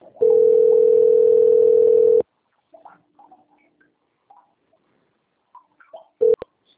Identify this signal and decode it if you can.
Ringback tone (North America), 440 + 480 Hz.